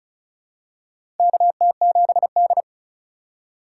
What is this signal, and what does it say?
Morse code, heard as a keyed tone, about 36 wpm, 700 Hz. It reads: KT7D
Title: USA Call Signs - 4 Characters - 4x Character Spacing - 35wpm